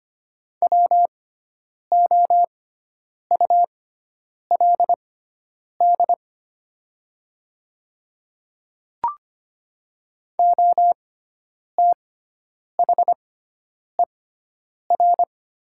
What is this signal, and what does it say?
Morse code, heard as a keyed tone, about 25 wpm, spaced as for 9 wpm, 700 Hz. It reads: WOULD OTHER
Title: 6x Character Spacing - Top 200 Words - 25wpm